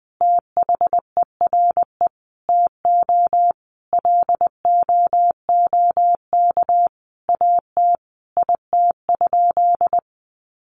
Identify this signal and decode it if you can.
Morse code, heard as a keyed tone, about 20 wpm, 700 Hz. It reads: THERE TO LOOK AT IT?
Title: Sentences from Top 100 Words 20wpm